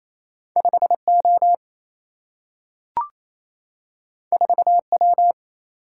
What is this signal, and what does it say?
Morse code, heard as a keyed tone, about 28 wpm, 700 Hz. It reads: 5 O 4 W
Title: Single-Letter Single-Number 28wpm